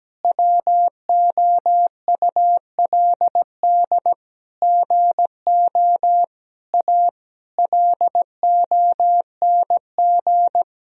Morse code, heard as a keyed tone, about 17 wpm, 700 Hz. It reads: WOULD GO A LONG
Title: Sentences from Top 1000 Words - Review Part 1 - 17wpm